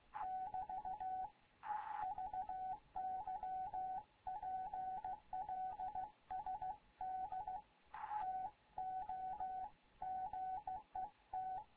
Morse code, heard as a keyed tone, about 15 wpm, 750 Hz. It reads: <BT> VY PLSD TO GET
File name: Morze-HA2MN / hangp / 2 11 GYAKORLAT